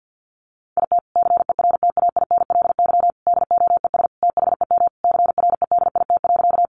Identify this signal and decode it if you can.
Morse code, heard as a keyed tone, about 50 wpm, 700 Hz. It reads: IT CERTAINLY DOES THEM CREDIT.
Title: Sentences from Top 1000 Words 50wpm